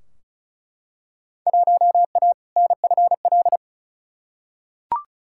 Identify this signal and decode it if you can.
Morse code, heard as a keyed tone, about 36 wpm, 700 Hz. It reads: 1A NFL
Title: Field Day Exchanges Encore - 35wpm